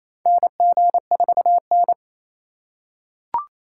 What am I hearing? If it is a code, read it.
Morse code, heard as a keyed tone, about 28 wpm, 700 Hz. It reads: NG4N
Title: USA Call Signs - 4 Characters - 4x Character Spacing - 28wpm